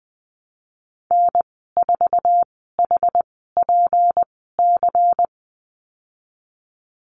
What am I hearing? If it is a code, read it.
Morse code, heard as a keyed tone, about 20 wpm, spaced as for 15 wpm, 700 Hz. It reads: N4HPC
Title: 167 Lesson 27g - Number 7 - Call Signs 20wpm